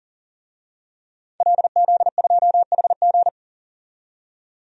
Morse code, heard as a keyed tone, about 40 wpm, 700 Hz. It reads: LZ2HG
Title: DXCC-100 Call Signs - 5 Characters - 40wpm